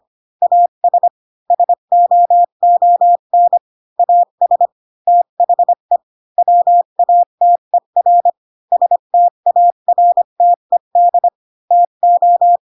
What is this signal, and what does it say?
Morse code, heard as a keyed tone, about 25 wpm, 700 Hz. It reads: AS SOON AS THE WATER STARTED TO
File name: Pinocchio / Pinocchio ch5 20 25 ik1hge